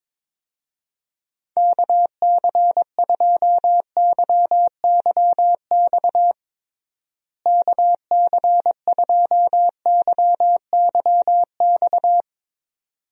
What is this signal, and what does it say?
Morse code, heard as a keyed tone, about 22 wpm, 700 Hz. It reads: KC2YYX KC2YYX
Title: USA Call Signs - 6 Characters - Repeated 2x - 22wpm